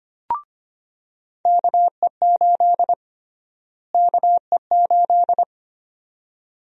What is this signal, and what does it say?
Morse code, heard as a keyed tone, about 25 wpm, 700 Hz. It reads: KE8 KE8
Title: USA Call Sign Prefixes - Repeated 2x - 25wpm